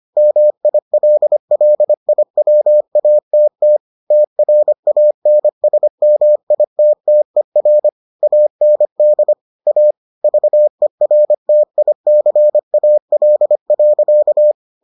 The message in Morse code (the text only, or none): MILLIWATT TRANSMITTER AND A VERTICAL.